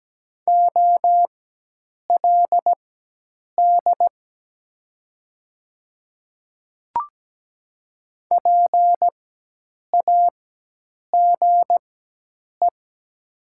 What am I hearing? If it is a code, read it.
Morse code, heard as a keyed tone, about 17 wpm, spaced as for 8 wpm, 700 Hz. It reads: OLD PAGE
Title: 4x Character Spacing - Top 300 Words - 17wpm